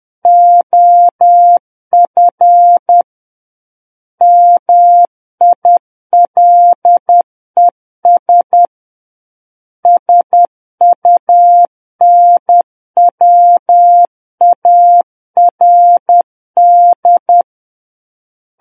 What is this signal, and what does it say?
Morse code, heard as a keyed tone, about 10 wpm, 700 Hz. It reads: OF MILES SUNWARD
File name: Book One / 01-Chapter 1 - 10 WPM